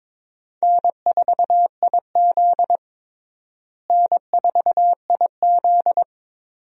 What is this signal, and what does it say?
Morse code, heard as a keyed tone, about 22 wpm, 700 Hz. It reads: N4IZ N4IZ
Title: USA Call Signs - 4 Characters - Repeated 2x - 22wpm